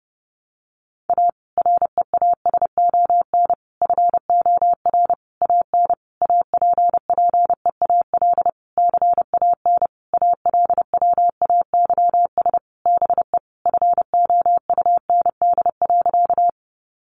Morse code, heard as a keyed tone, about 30 wpm, 700 Hz. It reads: A REASON FOR AN APPEAL CAN ALWAYS BE FOUND.
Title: Sentences from Top 1300 Words 30wpm